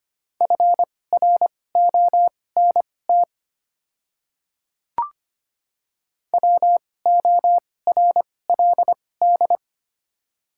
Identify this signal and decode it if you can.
Morse code, heard as a keyed tone, about 25 wpm, spaced as for 18 wpm, 700 Hz. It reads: FRONT WORLD
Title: 2x Character Spacing - Top 500 Words - 25wpm